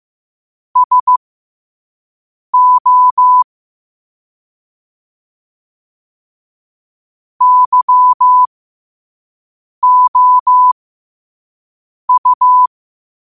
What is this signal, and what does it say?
Morse code, heard as a keyed tone, about 15 wpm, spaced as for 5 wpm, 1000 Hz. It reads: SO YOU
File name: qotdfiles / 5wpm